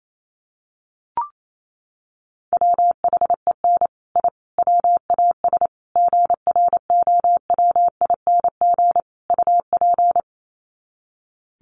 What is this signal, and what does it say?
Morse code, heard as a keyed tone, about 28 wpm, 700 Hz. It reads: WHEN I WAS GROWING UP
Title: Sets of 5 Words Encore 28wpm